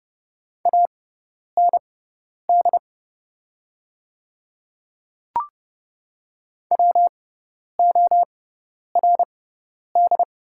Morse code, heard as a keyed tone, about 30 wpm, spaced as for 10 wpm, 700 Hz. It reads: AND WORD
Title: 6x Character Spacing - Top 100 Words - 30wpm